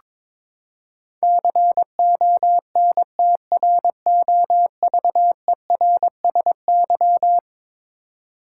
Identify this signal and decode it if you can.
Morse code, heard as a keyed tone, about 21 wpm, 700 Hz. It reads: CONTROVERSY
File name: practice / 22wpm / Eleven Letter Words - 22wpm